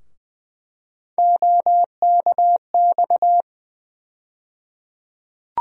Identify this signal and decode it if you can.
Morse code, heard as a keyed tone, about 20 wpm, 700 Hz. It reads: OKX
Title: USA Call Sign Postfixes - 3 Letters 20wpm